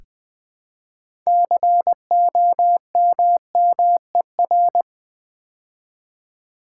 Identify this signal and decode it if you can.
Morse code, heard as a keyed tone, about 20 wpm, 700 Hz. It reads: COMMER
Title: Build A Word - 2 Letters At A Time - 8-Letter Words - 20wpm